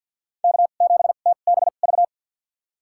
Morse code, heard as a keyed tone, about 51 wpm, spaced as for 36 wpm, 700 Hz. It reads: K7TBV
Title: USA Call Signs - 5 Characters - 2x Character Spacing - 50wpm